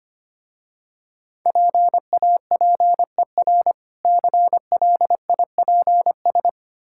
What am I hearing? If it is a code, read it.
Morse code, heard as a keyed tone, about 25 wpm, 700 Hz. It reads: PAPER CLIPS
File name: practice / 25wpm / Difficult Words with Hints - 25wpm